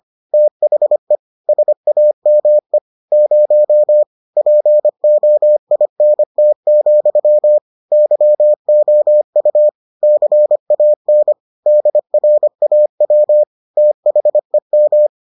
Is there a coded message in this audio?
Yes: Morse code from 0.3 to 15.1 s, heard as a keyed tone, about 25 wpm, 600 Hz. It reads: THE SAME 0 POINT, YOU CAN DRAW THEM